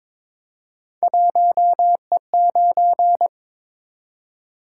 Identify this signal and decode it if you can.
Morse code, heard as a keyed tone, about 23 wpm, 700 Hz. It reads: 1E9
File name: practice / 22wpm / Three Letter-Number Combinations - 22wpm